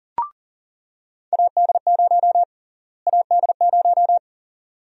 Morse code, heard as a keyed tone, about 40 wpm, 700 Hz. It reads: AD0 AD0
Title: USA Call Sign Prefixes - Repeated 2x - 40wpm